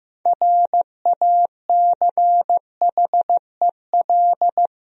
Morse code, heard as a keyed tone, about 15 wpm, 700 Hz. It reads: RACHEL
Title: USA Common Names - 15wpm